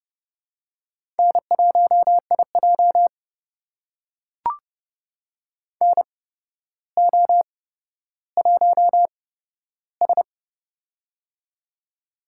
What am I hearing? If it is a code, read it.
Morse code, heard as a keyed tone, about 30 wpm, spaced as for 13 wpm, 700 Hz. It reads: N1IJ NO1S
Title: USA Call Signs - 4 Characters - 8x Character Spacing - 30wpm